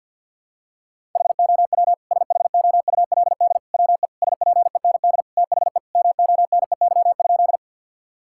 Morse code, heard as a keyed tone, about 47 wpm, 700 Hz. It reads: HOW SHOULD WE SPEND THE MONEY?
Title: Sentences from Top 1000 Words - Review Part 5 - 50wpm